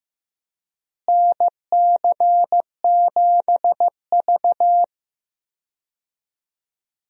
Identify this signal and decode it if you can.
Morse code, heard as a keyed tone, about 16 wpm, 700 Hz. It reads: NC7V
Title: USA Call Signs - 4 Characters - Repeated 2x - 15wpm